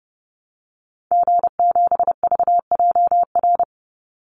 Morse code, heard as a keyed tone, about 30 wpm, 700 Hz. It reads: G7VJR
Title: Call Signs Repeated 2x - 30wpm